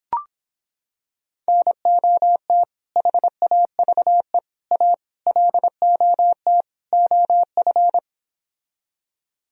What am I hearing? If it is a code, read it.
Morse code, heard as a keyed tone, about 26 wpm, 700 Hz. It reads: NOT HAVE A LOT OF